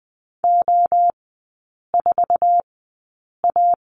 Morse code, heard as a keyed tone, about 20 wpm, spaced as for 8 wpm, 700 Hz. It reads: O4A
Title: 047 Lesson 9e - Number 4 - Three Characters 20wpm